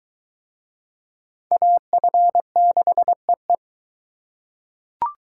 Morse code, heard as a keyed tone, about 20 wpm, 700 Hz. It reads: AF6EE